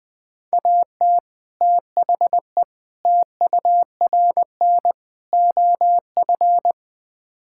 Morse code, heard as a keyed tone, about 20 wpm, 700 Hz. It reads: AT THE TURN OF
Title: Build A Phrase - 5-Word Phrases - 20wpm